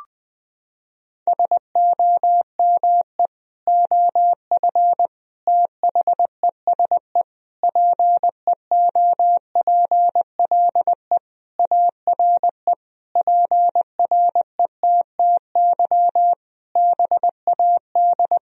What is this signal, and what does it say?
Morse code, heard as a keyed tone, about 21 wpm, 700 Hz. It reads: SOME OF THESE PEOPLE ARE PRETTY BAD
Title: Sentences from Top 1000 Words - Review Part 3 - 20wpm